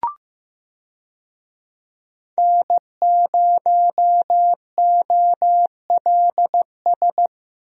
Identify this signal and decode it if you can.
Morse code, heard as a keyed tone, about 15 wpm, 700 Hz. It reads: N0OLS
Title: USA Call Signs - 5 Characters - 15wpm